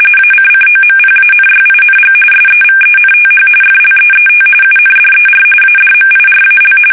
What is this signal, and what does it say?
RTTY, about 45.45 baud. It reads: ACCOMPANYING SEVENTH FLEET UNDER ADMIRAL
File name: fsk850 / battleshipnj